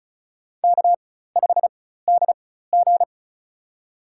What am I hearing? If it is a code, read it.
Morse code, heard as a keyed tone, about 36 wpm, spaced as for 16 wpm, 700 Hz. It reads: K5DG
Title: USA Call Signs - 4 Characters - 4x Character Spacing - 35wpm